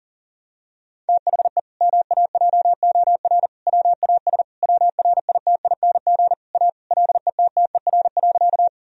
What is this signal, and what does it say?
Morse code, heard as a keyed tone, about 40 wpm, 700 Hz. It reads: THE MAJOR WAS WRITING A LETTER.
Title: Sentences from Top 1000 Words - Review Part 4 - 40wpm